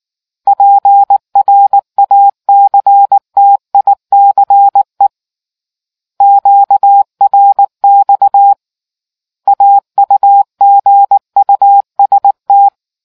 Morse code, heard as a keyed tone, about 19 wpm, 800 Hz. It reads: PRACTICE QRX AUGUST